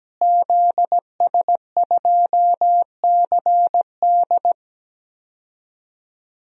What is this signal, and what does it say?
Morse code, heard as a keyed tone, about 17 wpm, 700 Hz. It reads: ZS2CD
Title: DXCC-100 Call Signs - 5 Characters - Repeated 2x - 17wpm